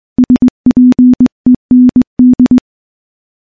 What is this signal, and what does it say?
Morse code, heard as a keyed tone, about 21 wpm, 250 Hz. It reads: SPEND
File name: morse / morse-training-20WPM